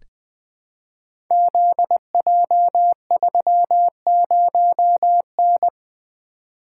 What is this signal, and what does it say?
Morse code, heard as a keyed tone, about 20 wpm, 700 Hz. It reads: ZJ30N
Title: Code Groups - 5 Letter:Numbers - 20wpm